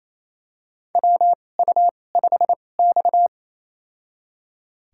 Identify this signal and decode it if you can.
Morse code, heard as a keyed tone, about 28 wpm, spaced as for 20 wpm, 700 Hz. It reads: WU5X
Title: USA Call Signs - 2x Character Spacing - 28wpm